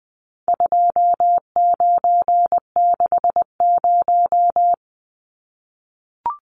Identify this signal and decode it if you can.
Morse code, heard as a keyed tone, about 20 wpm, 700 Hz. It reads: 2960
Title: Numbers - 4-Digit 20wpm